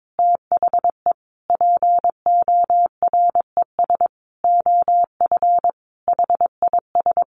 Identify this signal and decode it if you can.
Morse code, heard as a keyed tone, about 22 wpm, 700 Hz. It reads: THE PORES OF HIS
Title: A Sound Of Thunder 22wpm